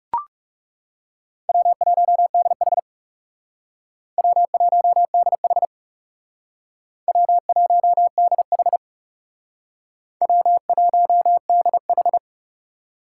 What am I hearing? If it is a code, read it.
Morse code, heard as a keyed tone, about 37 wpm, 700 Hz. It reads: W1DH W1DH W1DH W1DH
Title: Speed Racing - USA 1x2 Call Signs 30wpm